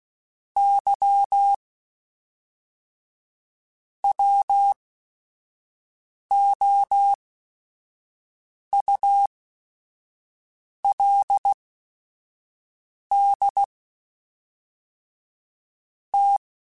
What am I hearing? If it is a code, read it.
Morse code, heard as a keyed tone, about 16 wpm, spaced as for 5 wpm, 800 Hz. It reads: Y WOULD T
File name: Technical Subjects / Technical Subjects-mp3s / AND Gate @5